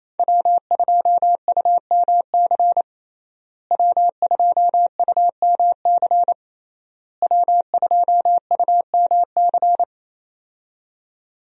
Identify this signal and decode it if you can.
Morse code, heard as a keyed tone, about 28 wpm, 700 Hz. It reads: W2UMC W2UMC W2UMC
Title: USA Call Signs - 5 Characters - Repeated 3x - 28wpm